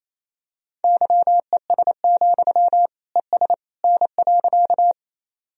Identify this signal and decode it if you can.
Morse code, heard as a keyed tone, about 28 wpm, 700 Hz. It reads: YES, ES N.
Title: Sentences from Top 500 Words and Abbreviations 28wpm